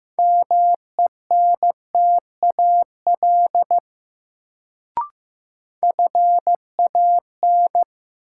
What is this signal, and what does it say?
Morse code, heard as a keyed tone, about 15 wpm, 700 Hz. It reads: MENTAL FAN
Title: Sentences from Top 1000 Words - Review Part 4 - New Word Drill - 15wpm